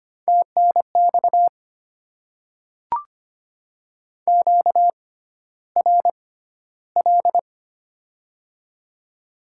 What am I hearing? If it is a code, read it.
Morse code, heard as a keyed tone, about 25 wpm, spaced as for 13 wpm, 700 Hz. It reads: TNX QRL
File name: Top 20 QSO Elements - 6x Character Spacing - 25wpm